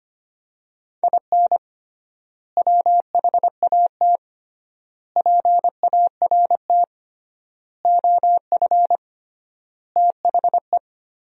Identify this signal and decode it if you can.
Morse code, heard as a keyed tone, about 25 wpm, 700 Hz. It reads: IN WHAT PART OF THE
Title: 3x Word-Spacing - Sentences from Top 200 Words - 25wpm